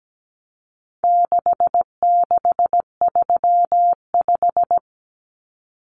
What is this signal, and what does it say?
Morse code, heard as a keyed tone, about 17 wpm, 700 Hz. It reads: 6635
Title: Numbers - 4-Digit 17wpm